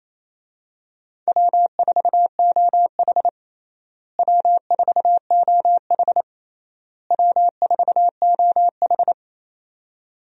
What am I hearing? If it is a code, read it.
Morse code, heard as a keyed tone, about 28 wpm, 700 Hz. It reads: W4OH W4OH W4OH
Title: USA Call Signs - 4 Characters - Repeated 3x - 28wpm